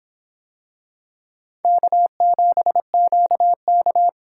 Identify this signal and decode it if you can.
Morse code, heard as a keyed tone, about 23 wpm, 700 Hz. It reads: K7QK